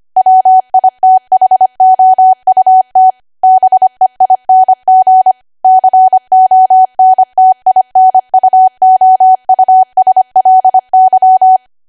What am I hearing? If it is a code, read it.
Morse code, heard as a keyed tone, about 25 wpm, 750 Hz. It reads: WITHOUT BEING CONTINUOUSLY
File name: cwa / W1AW-Level3 / 25WPM